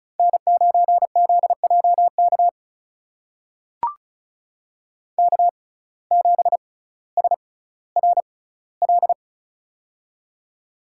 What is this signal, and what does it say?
Morse code, heard as a keyed tone, about 36 wpm, spaced as for 18 wpm, 700 Hz. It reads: N9ZJK K7SRL
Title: USA Call Signs - 5 Characters - 6x Character Spacing - 35wpm